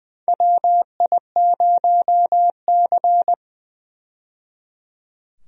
Morse code, heard as a keyed tone, about 20 wpm, 700 Hz. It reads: WI0C
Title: USA Call Signs - 4 Characters - 20wpm